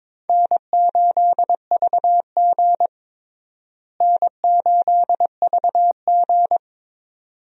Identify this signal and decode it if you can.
Morse code, heard as a keyed tone, about 21 wpm, 700 Hz. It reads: N8VG N8VG
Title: USA Call Signs - 4 Characters - Repeated 2x - 22wpm